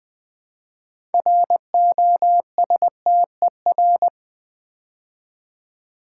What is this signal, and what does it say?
Morse code, heard as a keyed tone, about 20 wpm, 700 Hz. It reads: ROSTER
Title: Build A Word - 2 Letters At A Time - 6-Letter Words - 20wpm